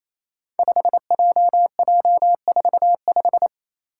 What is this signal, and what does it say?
Morse code, heard as a keyed tone, about 28 wpm, 700 Hz. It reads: 5JJ45
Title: Five Challenging Letter-Number Combinations 28wpm